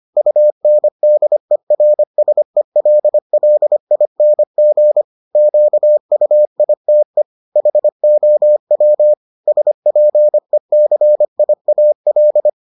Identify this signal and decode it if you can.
Morse code, heard as a keyed tone, about 25 wpm, 600 Hz. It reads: UNDERSELLING QUITE HOW SPECIAL